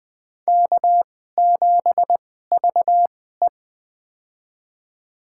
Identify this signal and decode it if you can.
Morse code, heard as a keyed tone, about 21 wpm, spaced as for 15 wpm, 700 Hz. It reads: K7VE
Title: USA Call Signs - 4 Characters - 2x Character Spacing - 20wpm